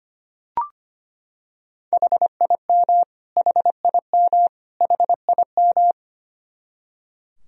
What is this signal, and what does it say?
Morse code, heard as a keyed tone, about 25 wpm, 700 Hz. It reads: HIM HIM HIM
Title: Top 100 Words Repeated 25wpm